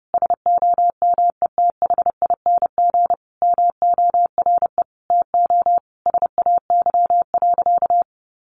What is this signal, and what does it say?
Morse code, heard as a keyed tone, about 30 wpm, 700 Hz. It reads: SOMETHING MORE TO SAY.
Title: Sentences from Top 600 Words 30wpm